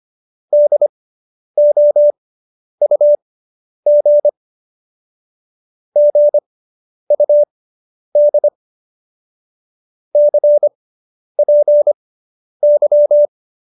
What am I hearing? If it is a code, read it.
Morse code, heard as a keyed tone, about 25 wpm, spaced as for 10 wpm, 590 Hz. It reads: DOUG GUD CPY